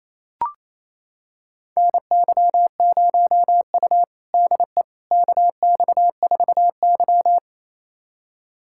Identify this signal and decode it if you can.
Morse code, heard as a keyed tone, about 28 wpm, 700 Hz. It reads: NY0U DE KX4Y
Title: USA Callsign DE Callsign - 4 Characters - 28wpm